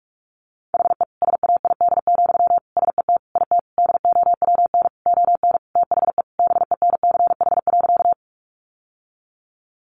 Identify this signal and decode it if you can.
Morse code, heard as a keyed tone, about 44 wpm, 700 Hz. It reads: HE SAID, SET IT DOWN ON THE BENCH.
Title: Sentences from Top 1400 Words 45wpm